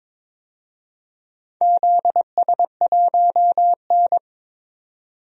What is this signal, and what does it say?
Morse code, heard as a keyed tone, about 22 wpm, 700 Hz. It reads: ZS1N
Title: DXCC-100 Call Signs - 4 Characters - 22wpm